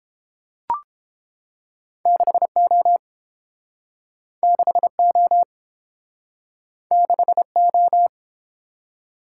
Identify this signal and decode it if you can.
Morse code, heard as a keyed tone, about 29 wpm, 700 Hz. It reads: 6O 6O 6O